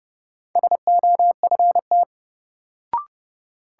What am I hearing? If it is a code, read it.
Morse code, heard as a keyed tone, about 30 wpm, 700 Hz. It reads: SOFT
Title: Sentences from Top 1500 Words - New Word Drill Alt - 30wpm